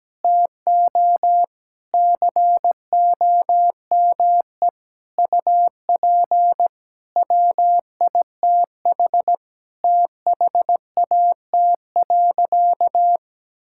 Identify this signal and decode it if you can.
Morse code, heard as a keyed tone, about 17 wpm, 700 Hz. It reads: TO COME UP WITH THAT.
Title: Sentences from Top 1000 Words - Review Part 5 - 17wpm